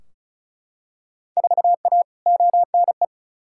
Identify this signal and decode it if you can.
Morse code, heard as a keyed tone, about 36 wpm, 700 Hz. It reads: 4A ONE
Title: Field Day Exchanges Encore - 35wpm